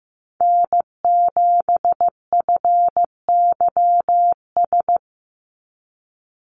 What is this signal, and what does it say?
Morse code, heard as a keyed tone, about 15 wpm, 700 Hz. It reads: N7FYS
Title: USA Call Signs 15wpm